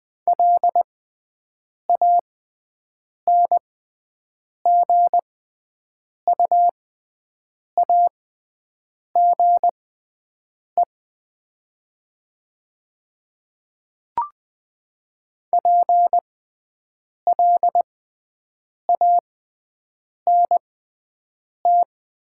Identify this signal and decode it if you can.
Morse code, heard as a keyed tone, about 20 wpm, spaced as for 7 wpm, 700 Hz. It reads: LANGUAGE PLANT
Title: 6x Character Spacing - Top 500 Words - 20wpm